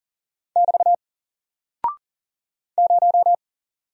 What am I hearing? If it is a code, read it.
Morse code, heard as a keyed tone, about 40 wpm, 700 Hz. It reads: <BT> 0
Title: Single Character (Letter:Number:Punctuation:Prosign) Alt 40wpm